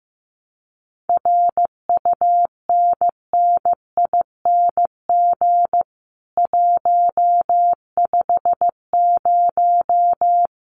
Morse code, heard as a keyed tone, about 15 wpm, 700 Hz. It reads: RUNNING 150
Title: Common QSO Phrases 15wpm